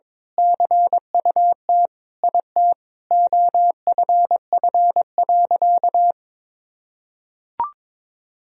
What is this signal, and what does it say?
Morse code, heard as a keyed tone, about 21 wpm, 700 Hz. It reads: CUT IT OFF.
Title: Sentences from Top 1000 Words - Review Part 1 - 22wpm